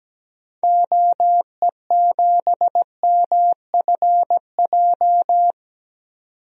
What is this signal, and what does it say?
Morse code, heard as a keyed tone, about 17 wpm, 700 Hz. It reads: OE7MFJ
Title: DXCC-100 Call Signs - 6 Characters - 17wpm